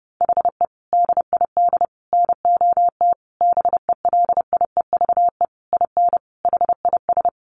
Morse code, heard as a keyed tone, about 30 wpm, 700 Hz. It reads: HE DID NOT BELIEVE IN HIS
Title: Sentences from Top 1300 Words 30wpm